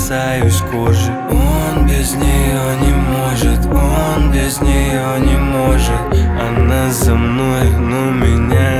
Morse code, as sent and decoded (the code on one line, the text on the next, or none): none